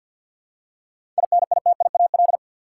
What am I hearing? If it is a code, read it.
Morse code, heard as a keyed tone, about 46 wpm, 700 Hz. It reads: INITIAL
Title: Seven Letter Words - 50wpm